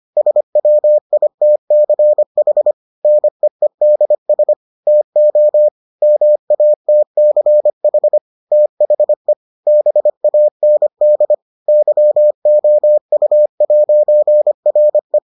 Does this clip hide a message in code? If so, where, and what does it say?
Yes: Morse code from 0.2 to 15.2 s, heard as a keyed tone, about 25 wpm, 600 Hz. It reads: SWITCH NEEDS TO MATCH THE BAND YOU'RE